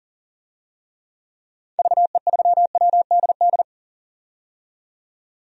Morse code, heard as a keyed tone, about 40 wpm, 700 Hz. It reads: VE3WDD
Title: DXCC-100 Call Signs - 6 Characters - 40wpm